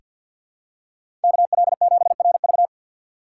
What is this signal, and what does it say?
Morse code, heard as a keyed tone, about 46 wpm, 700 Hz. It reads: KL7RV
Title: USA Call Signs - 4x Character Spacing - 50wpm